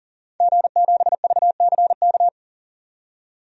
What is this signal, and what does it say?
Morse code, heard as a keyed tone, about 40 wpm, 700 Hz. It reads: G7VCK
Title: DXCC-100 Call Signs - 5 Characters - 40wpm